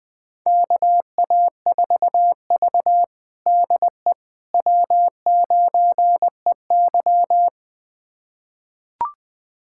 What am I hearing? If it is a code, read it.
Morse code, heard as a keyed tone, about 21 wpm, 700 Hz. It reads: KA4V DE W9EY
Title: USA Callsign DE Callsign - 4 Characters - 20wpm